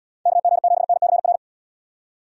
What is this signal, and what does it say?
Morse code, heard as a keyed tone, about 77 wpm, 700 Hz. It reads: KG7NCR